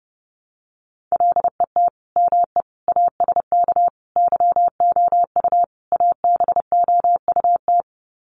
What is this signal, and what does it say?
Morse code, heard as a keyed tone, about 30 wpm, 700 Hz. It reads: LET ME ASK YOU ABOUT
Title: Sets of 5 Words 30wpm